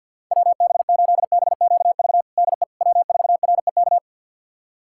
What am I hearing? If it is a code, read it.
Morse code, heard as a keyed tone, about 50 wpm, 700 Hz. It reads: WB8BQV DE W4REK
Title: USA Callsign DE Callsign 50wpm